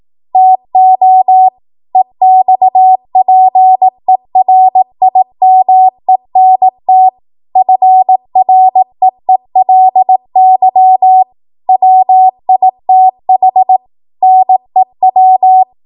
Morse code, heard as a keyed tone, about 18 wpm, 750 Hz. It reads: TO EXPERIMENT FREELY WITH NEW
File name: Morse / 18WPM